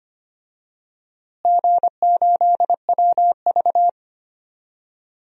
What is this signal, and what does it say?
Morse code, heard as a keyed tone, about 25 wpm, 700 Hz. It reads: G8WV
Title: DXCC-100 Call Signs - 4 Characters - Repeated 3x - 25wpm